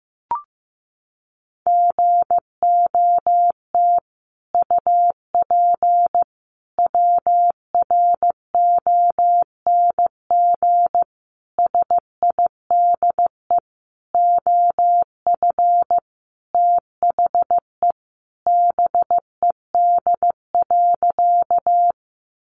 Morse code, heard as a keyed tone, about 15 wpm, 700 Hz. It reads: GOT UP WRONG SIDE OF THE BED.
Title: Sentences from Top 900 Words 15wpm